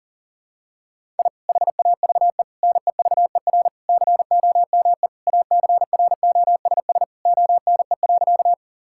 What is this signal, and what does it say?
Morse code, heard as a keyed tone, about 40 wpm, 700 Hz. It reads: I HAVE NEVER COME ACROSS ONE.
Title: Sentences from Top 1000 Words - Review Part 5 - 40wpm